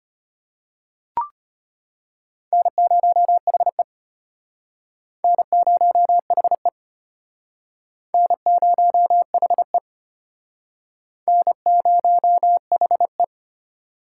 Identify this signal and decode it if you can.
Morse code, heard as a keyed tone, about 33 wpm, 700 Hz. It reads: N0HE N0HE N0HE N0HE